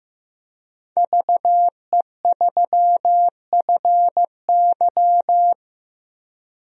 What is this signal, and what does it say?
Morse code, heard as a keyed tone, about 15 wpm, 700 Hz. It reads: VE3FY
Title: DXCC-100 Call Signs - 5 Characters - 15wpm